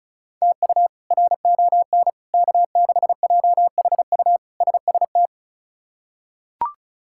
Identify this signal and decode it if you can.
Morse code, heard as a keyed tone, about 36 wpm, 700 Hz. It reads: TU RON K6JHU SST
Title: K1USN SST QSO Practice - 35wpm